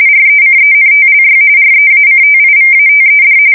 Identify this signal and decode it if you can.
RTTY, about 45.45 baud. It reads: RECENTLY REACTIVATED
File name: fsk170 / USS-New-Jersey